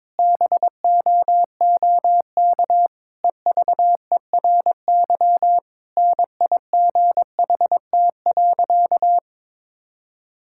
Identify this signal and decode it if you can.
Morse code, heard as a keyed tone, about 22 wpm, 700 Hz. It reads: BOOK EVERY NIGHT.
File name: Difficult Sentences with Hints - 22wpm